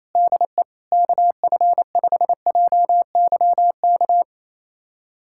Morse code, heard as a keyed tone, about 28 wpm, 700 Hz. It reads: DE KF5JYK
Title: USA Callsign DE Callsign 28wpm